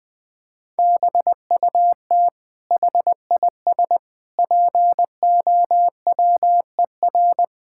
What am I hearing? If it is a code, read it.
Morse code, heard as a keyed tone, about 20 wpm, 700 Hz. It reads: BUT HIS POWER
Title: Sentences from Top 1000 Words - Review Part 6 - 20wpm